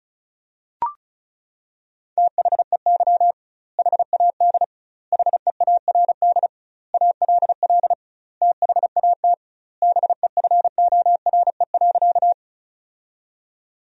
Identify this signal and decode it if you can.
Morse code, heard as a keyed tone, about 35 wpm, 700 Hz. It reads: THEY HAD HEARD ALL THAT BEFORE.
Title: 2x Word-Spacing - Sentences from Top 400 Words - 35wpm